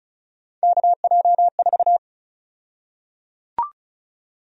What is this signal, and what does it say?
Morse code, heard as a keyed tone, about 36 wpm, 700 Hz. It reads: KJ4
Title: USA Call Sign Prefixes - Repeated 2x - 35wpm